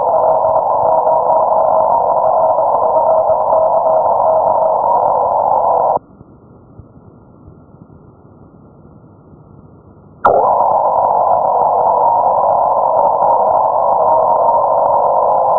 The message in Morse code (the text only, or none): none